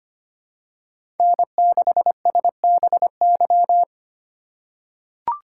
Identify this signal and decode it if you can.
Morse code, heard as a keyed tone, about 25 wpm, 700 Hz. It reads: N6SBY
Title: USA Call Signs - 5 Characters - 8x Character Spacing - 25wpm